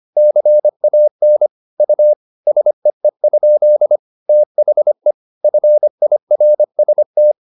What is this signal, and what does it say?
Morse code, heard as a keyed tone, about 25 wpm, 600 Hz. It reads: CAN U SEE? THE FIRST
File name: SS319 25